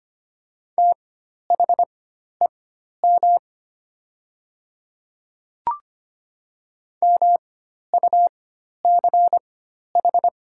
Morse code, heard as a keyed tone, about 25 wpm, spaced as for 12 wpm, 690 Hz. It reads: THEM MUCH
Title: 4x Character Spacing - Top 200 Words - 25wpm